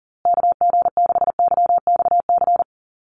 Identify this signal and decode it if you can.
Morse code, heard as a keyed tone, about 40 wpm, 700 Hz. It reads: KG6YXC
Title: USA Call Signs Repeated 2x 40wpm